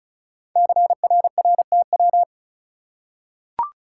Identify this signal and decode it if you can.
Morse code, heard as a keyed tone, about 36 wpm, 690 Hz. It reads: CRRTW
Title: Code Groups - 5 Letters - 35wpm